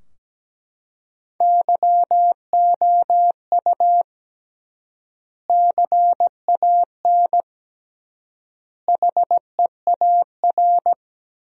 Morse code, heard as a keyed tone, about 17 wpm, 700 Hz. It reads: YOU CAN HEAR
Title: 3x Word-Spacing - Sentences from Top 500 Words - 17wpm